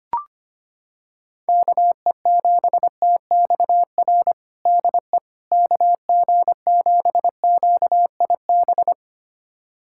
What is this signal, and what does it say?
Morse code, heard as a keyed tone, about 25 wpm, 700 Hz. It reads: KE7TXR DE KG7QIB
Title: USA Callsign DE Callsign - 6 Characters - 25wpm